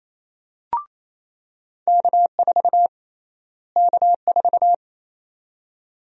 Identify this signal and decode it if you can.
Morse code, heard as a keyed tone, about 28 wpm, 700 Hz. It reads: K4 K4
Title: USA Call Sign Prefixes - Repeated 2x - 28wpm